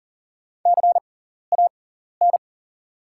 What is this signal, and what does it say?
Morse code, heard as a keyed tone, about 42 wpm, spaced as for 14 wpm, 690 Hz. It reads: CAN
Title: 6x Character Spacing - Top 100 Words - 40wpm